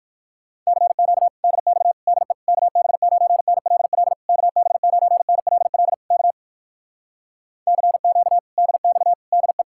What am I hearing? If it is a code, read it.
Morse code, heard as a keyed tone, about 66 wpm, spaced as for 58 wpm, 700 Hz. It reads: CQ DX DE KB9NLL KB9NLL K CQ DX DE